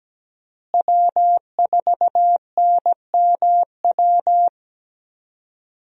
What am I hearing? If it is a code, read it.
Morse code, heard as a keyed tone, about 17 wpm, 700 Hz. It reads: W4NMW
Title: USA Call Signs - 5 Characters - Repeated 3x - 17wpm